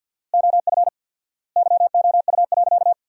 Morse code, heard as a keyed tone, about 50 wpm, 700 Hz. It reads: OF YOU.
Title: 4x Word-Spacing - Sentences from Top 200 Words - 50wpm